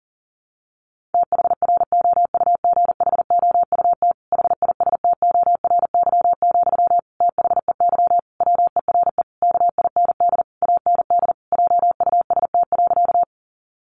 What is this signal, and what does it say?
Morse code, heard as a keyed tone, about 40 wpm, 700 Hz. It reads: THROUGHOUT HISTORY, THEY WERE KIND AND JUST.
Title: Sentences from Top 800 Words 40wpm